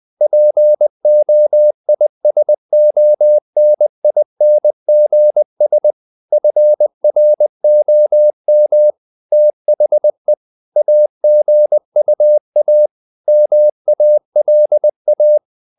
Morse code, heard as a keyed tone, about 20 wpm, 600 Hz. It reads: POISONINGS FROM THE AGUA MALA
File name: the old man and the sea / the old man and the sea part 1 20wpm